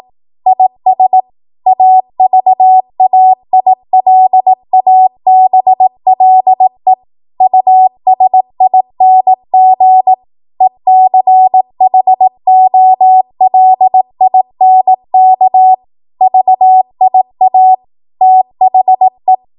Morse code, heard as a keyed tone, about 18 wpm, 750 Hz. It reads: IS AVAILABLE USING ECHOLINK VIA THE
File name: Archive / 18 WPM / 18WPM